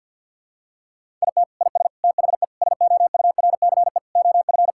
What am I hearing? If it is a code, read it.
Morse code, heard as a keyed tone, about 53 wpm, 700 Hz. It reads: IT IS THE SOURCE OF
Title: Sentences from Top 1000 Words - Review Part 5 - 50wpm